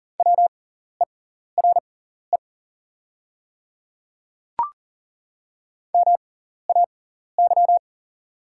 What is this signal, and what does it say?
Morse code, heard as a keyed tone, about 40 wpm, spaced as for 14 wpm, 700 Hz. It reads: WERE MAY
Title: 6x Character Spacing - Top 500 Words - 40wpm